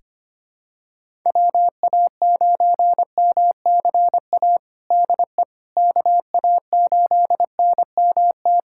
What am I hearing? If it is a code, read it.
Morse code, heard as a keyed tone, about 25 wpm, 700 Hz. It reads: WA9MCA DE KA8NMT
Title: USA Callsign DE Callsign - 6 Characters - 25wpm